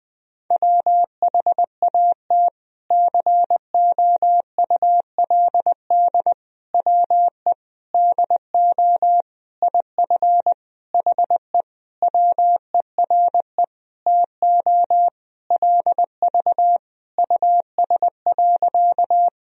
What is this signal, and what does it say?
Morse code, heard as a keyed tone, about 21 wpm, 700 Hz. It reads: WHAT COULD WE DO IF HE WERE TO LV US.
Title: Sentences from Top 400 Words and Abbreviations 20wpm